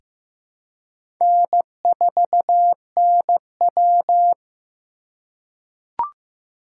Morse code, heard as a keyed tone, about 15 wpm, 700 Hz. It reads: N4NW
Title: K1USN SST QSO Practice - 15wpm